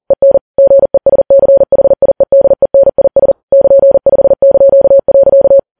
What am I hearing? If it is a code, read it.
Morse code, heard as a keyed tone, about 40 wpm, 550 Hz. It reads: EN GESCHIEDENIS <KN>5).
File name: Files Web tls / Morse on9fv / morse40kps